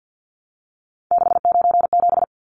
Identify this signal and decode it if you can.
Morse code, heard as a keyed tone, about 50 wpm, 700 Hz. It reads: 697
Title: Numbers - 3-Digit 50wpm